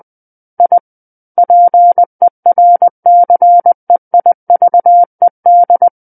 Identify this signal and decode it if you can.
Morse code, heard as a keyed tone, about 20 wpm, 700 Hz. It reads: I PERCEIVED
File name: 20 WPM